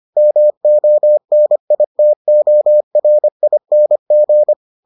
Morse code, heard as a keyed tone, about 25 wpm, 600 Hz. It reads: MONITORING